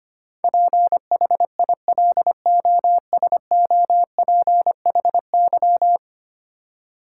Morse code, heard as a keyed tone, about 25 wpm, 700 Hz. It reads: PHILOSOPHY
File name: 10 Letter Words - First 3 Letters Given - 25wpm